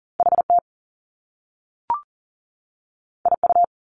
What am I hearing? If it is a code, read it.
Morse code, heard as a keyed tone, about 38 wpm, 700 Hz. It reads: HT IU
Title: Top Two Letter Combinations in Words 40wpm